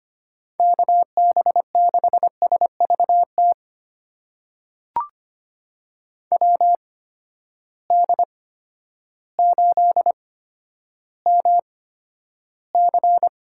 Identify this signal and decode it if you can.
Morse code, heard as a keyed tone, about 25 wpm, spaced as for 11 wpm, 700 Hz. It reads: KB6SVT WD8MC
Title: USA Call Signs - 6 Characters - 8x Character Spacing - 25wpm